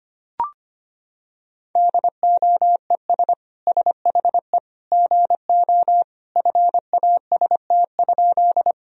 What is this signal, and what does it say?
Morse code, heard as a keyed tone, about 25 wpm, 700 Hz. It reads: DOES SHE GO FAST?
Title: Sentences from Top 1000 Words - Review Part 2 - 25wpm